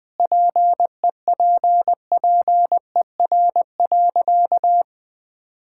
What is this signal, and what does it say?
Morse code, heard as a keyed tone, about 20 wpm, 700 Hz. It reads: PEPPER.
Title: Sentences with 5-Word Phrases - Spoken Beginning - 20wpm